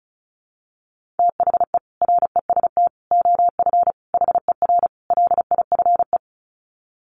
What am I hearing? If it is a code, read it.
Morse code, heard as a keyed tone, about 34 wpm, 700 Hz. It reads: THE REST OF HER LIFE
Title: Sets of 5 Words Encore 35wpm